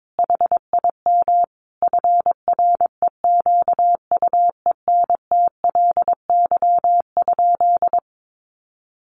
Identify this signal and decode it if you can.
Morse code, heard as a keyed tone, about 22 wpm, 700 Hz. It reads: HIM FREQUENTLY?
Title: Sentences from Top 1300 Words 22wpm